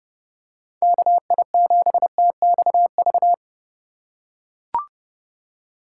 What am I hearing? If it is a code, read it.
Morse code, heard as a keyed tone, about 30 wpm, 700 Hz. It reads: KI7TXV
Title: USA Call Signs - 4x Character Spacing - 30wpm